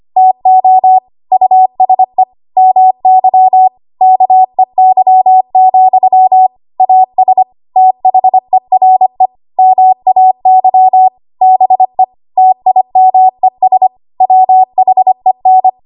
Morse code, heard as a keyed tone, about 25 wpm, 750 Hz. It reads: TO USE MY KEY, AS THERE MAY BE TIMES WHEN
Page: 25 WPM Code Archive